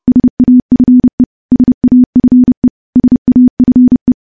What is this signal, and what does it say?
Morse code, heard as a keyed tone, about 30 wpm, 250 Hz. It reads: SAFE SAFE SAFE
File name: morse / morse-training-30WPM